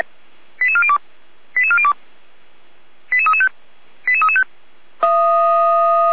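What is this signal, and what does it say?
Five-tone selective call (ZVEI), heard as quick runs of five tones, tones of about 70 ms. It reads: 88402 88402 88335 88335
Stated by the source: ZVEI five-tone selective call, code 89229, received with an SDR.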